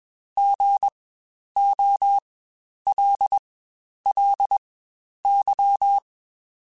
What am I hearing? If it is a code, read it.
Morse code, heard as a keyed tone, about 20 wpm, spaced as for 10 wpm, 800 Hz. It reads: GOLLY
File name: Spaced Alphas / Spaced Alphas mp3s / Gs sp @20